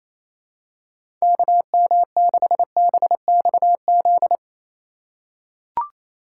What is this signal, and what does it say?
Morse code, heard as a keyed tone, about 28 wpm, 700 Hz. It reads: KM6BXZ
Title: USA Call Signs - 6 Characters - 2x Character Spacing - 28wpm